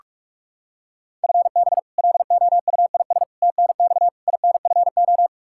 Morse code, heard as a keyed tone, about 46 wpm, 700 Hz. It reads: FB LOUIS TNX INFO
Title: Practice QSO - Exchange 3 - More Info - 45wpm